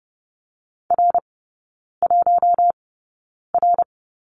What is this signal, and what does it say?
Morse code, heard as a keyed tone, about 30 wpm, spaced as for 9 wpm, 690 Hz. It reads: R1R
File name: Lesson 16e - Letter c - Three Characters 30wpm